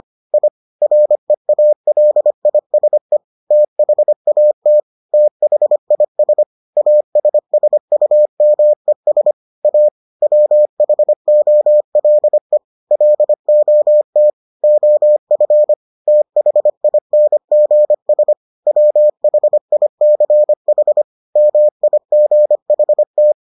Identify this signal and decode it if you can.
Morse code, heard as a keyed tone, about 25 wpm, 600 Hz. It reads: I REALISE THAT THIS ASSUMES A WHOLE LOT OF THINGS WHICH MIGHT